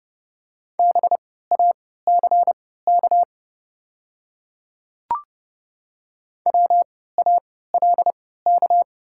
Morse code, heard as a keyed tone, about 30 wpm, spaced as for 17 wpm, 700 Hz. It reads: BACK WALK
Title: 3x Character Spacing - Top 300 Words - 30wpm